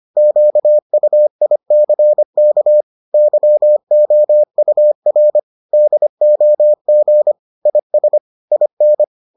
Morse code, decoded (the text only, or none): QUICK YOUR DOG IS IN